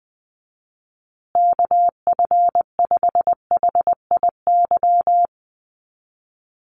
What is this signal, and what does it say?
Morse code, heard as a keyed tone, about 20 wpm, 700 Hz. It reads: KF5HIY
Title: USA Call Signs Repeated 3x 20wpm